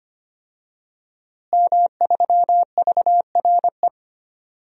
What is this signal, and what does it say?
Morse code, heard as a keyed tone, about 25 wpm, 700 Hz. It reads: M3VRE
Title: DXCC-100 Call Signs - 5 Characters - Repeated 3x - 25wpm